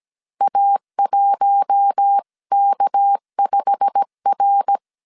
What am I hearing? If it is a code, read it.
Morse code, heard as a keyed tone, about 17 wpm, 800 Hz. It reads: A1K5R
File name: CW / DET / 1 9 1500C S C 16wpm 2909